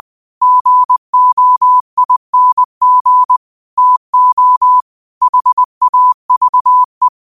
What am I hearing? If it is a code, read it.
Morse code, heard as a keyed tone, about 20 wpm, 1000 Hz. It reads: GOING TO HAVE